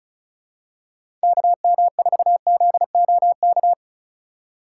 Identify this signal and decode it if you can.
Morse code, heard as a keyed tone, about 36 wpm, 700 Hz. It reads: KM4ZOK
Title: USA Call Signs - 6 Characters - 2x Character Spacing - 35wpm